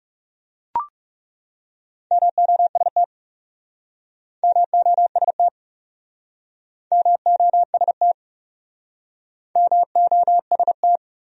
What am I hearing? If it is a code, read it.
Morse code, heard as a keyed tone, about 42 wpm, 700 Hz. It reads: MOST MOST MOST MOST